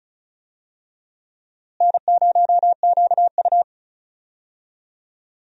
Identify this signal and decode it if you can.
Morse code, heard as a keyed tone, about 36 wpm, 700 Hz. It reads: N0QU
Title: USA Call Signs - 4 Characters - Repeated 2x - 35wpm